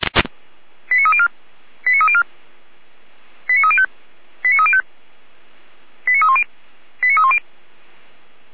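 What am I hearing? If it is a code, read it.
Five-tone selective call (ZVEI), heard as quick runs of five tones, tones of about 70 ms. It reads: 89304 89304 89305 89305 89310 89310
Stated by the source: ZVEI five-tone selective call 89250 received by SDR, logged on the Feuerwehr Orlamünde fire brigade server.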